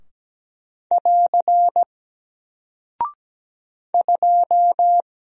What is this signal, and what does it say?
Morse code, heard as a keyed tone, about 17 wpm, 700 Hz. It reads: <AR> 2
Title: Single Character (Letter:Number:Punctuation:Prosign) Alt 17wpm